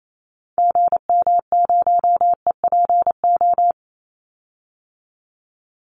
Morse code, heard as a keyed tone, about 28 wpm, 700 Hz. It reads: GM0EPO
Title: Call Signs Repeated 2x - 28wpm